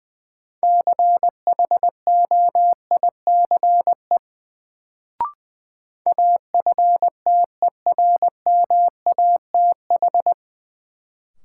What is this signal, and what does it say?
Morse code, heard as a keyed tone, about 20 wpm, 700 Hz. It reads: CHOICE AFTERMATH
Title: Sets of 5 Words - New Word Drill - 20wpm